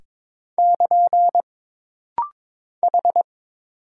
Morse code, heard as a keyed tone, about 22 wpm, 700 Hz. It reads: <KN> H
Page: This rapid-fire timing is geared to push your ICR (Instant Character Recognition) skills.